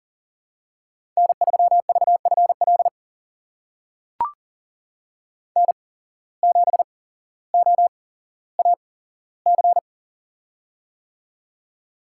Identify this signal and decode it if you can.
Morse code, heard as a keyed tone, about 38 wpm, spaced as for 17 wpm, 700 Hz. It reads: N3VFL N7OAC
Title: USA Call Signs - 5 Characters - 8x Character Spacing - 40wpm